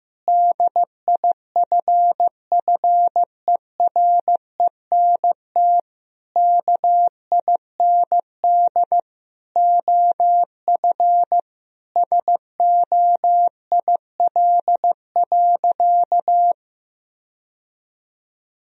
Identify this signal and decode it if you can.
Morse code, heard as a keyed tone, about 15 wpm, 700 Hz. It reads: DIFFERENT KIND OF SOIL.
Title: Sentences from Top 1000 Words - Review Part 4 - 15wpm